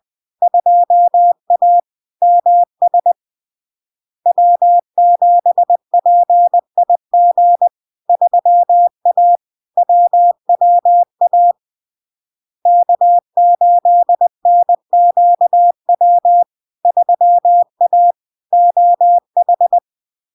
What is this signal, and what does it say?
Morse code, heard as a keyed tone, about 20 wpm, 700 Hz. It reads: 2A MS W7PIG 3A WWA K8NQW 3A OH
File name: LARK-FD-2022-Log-in-CW-20wpm-700-Hz